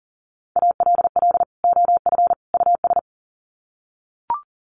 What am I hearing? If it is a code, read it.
Morse code, heard as a keyed tone, about 40 wpm, 700 Hz. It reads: ALL OF US
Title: Sets of 3 Words 40wpm